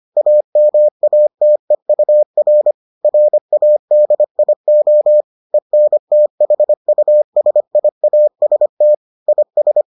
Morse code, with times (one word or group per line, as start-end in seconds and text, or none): 0.2-2.7 AMATEUR
3.0-5.2 RADIO
5.5-8.9 ENTHUSIAST
9.3-9.8 IS